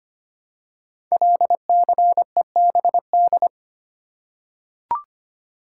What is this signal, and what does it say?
Morse code, heard as a keyed tone, about 25 wpm, 700 Hz. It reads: LCEBD
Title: Code Groups - 5 Letters - 25wpm